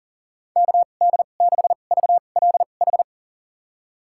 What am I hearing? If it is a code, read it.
Morse code, heard as a keyed tone, about 40 wpm, spaced as for 29 wpm, 700 Hz. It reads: KD6VLH
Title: USA Call Signs - 2x Character Spacing - 40wpm